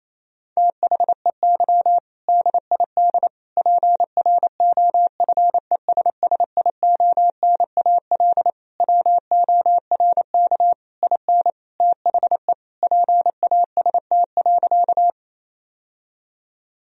Morse code, heard as a keyed tone, about 28 wpm, 700 Hz. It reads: THEY DID PROFESSIONAL WORK IN THE PAST.
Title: Sentences from Top 1000 Words - Review Part 4 - 28wpm